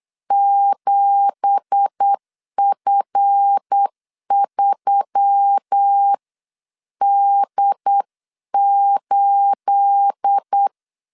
Morse code, heard as a keyed tone, about 8 wpm, 800 Hz. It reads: 7F3D8
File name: CW / DET / 1 9 1500C S C 8wpm 5653